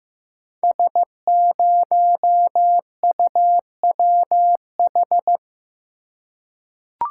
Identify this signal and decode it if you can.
Morse code, heard as a keyed tone, about 15 wpm, 700 Hz. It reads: S0UWH
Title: Code Groups - 5 Letter:Numbers - 15wpm